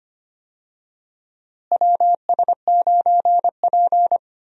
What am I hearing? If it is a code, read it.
Morse code, heard as a keyed tone, about 25 wpm, 700 Hz. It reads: WS9P
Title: USA Call Signs - 4 Characters - Repeated 2x - 25wpm